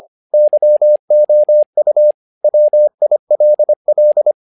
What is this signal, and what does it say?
Morse code, heard as a keyed tone, about 25 wpm, 600 Hz. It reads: YOU WILL
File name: learning center / tower trouble / tower trouble 25wpm